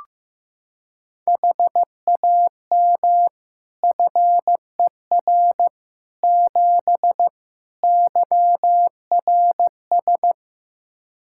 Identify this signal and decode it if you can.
Morse code, heard as a keyed tone, about 16 wpm, 700 Hz. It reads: HAM FER 7 YRS
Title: Practice QSO - Exchange 3 - More Info - 15wpm